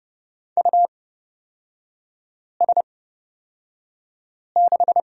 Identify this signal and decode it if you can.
Morse code, heard as a keyed tone, about 30 wpm, spaced as for 5 wpm, 700 Hz. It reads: US6
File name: Lesson 36e - BT - Three Characters 30wpm